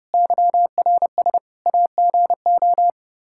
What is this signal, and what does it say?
Morse code, heard as a keyed tone, about 30 wpm, 700 Hz. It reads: YRS AGO
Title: Practice QSO - Exchange 3 - More Info - 30wpm